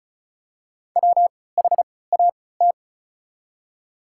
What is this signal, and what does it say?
Morse code, heard as a keyed tone, about 36 wpm, spaced as for 20 wpm, 700 Hz. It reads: WHAT
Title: 3x Character Spacing - Top 100 Words - 35wpm